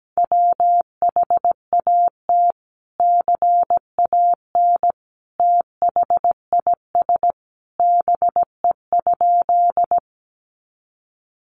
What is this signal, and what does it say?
Morse code, heard as a keyed tone, about 17 wpm, 700 Hz. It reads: WHAT CAN THIS BE?
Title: Sentences from Top 100 Words 17wpm